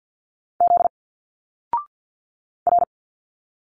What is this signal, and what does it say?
Morse code, heard as a keyed tone, about 50 wpm, 700 Hz. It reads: Z R
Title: Single Letters 50wpm